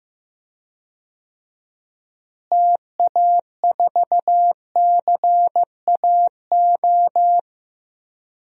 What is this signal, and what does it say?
Morse code, heard as a keyed tone, about 15 wpm, 700 Hz. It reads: TA4CAO
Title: DXCC-100 Call Signs - 6 Characters - Repeated 2x - 15wpm